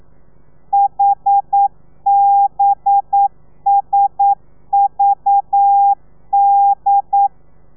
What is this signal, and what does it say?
Morse code, heard as a keyed tone, about 9 wpm, 800 Hz. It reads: HBSVD
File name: telegrafia / i2tfj / Lez4 40